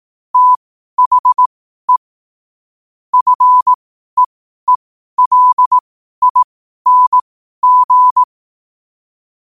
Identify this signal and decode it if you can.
Morse code, heard as a keyed tone, about 18 wpm, spaced as for 12 wpm, 1000 Hz. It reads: THE FEELING
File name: qotdfiles / 12wpm